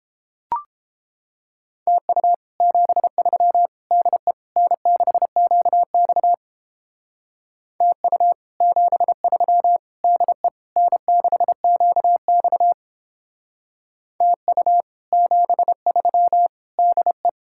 Morse code, heard as a keyed tone, about 31 wpm, 700 Hz. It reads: TU 73 DE N6QX TU 73 DE N6QX TU 73 DE